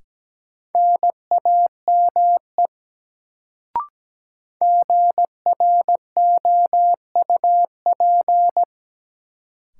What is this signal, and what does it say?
Morse code, heard as a keyed tone, about 17 wpm, 700 Hz. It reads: NAME GROUP
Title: Sets of 3 Words - New Word Drill - 17wpm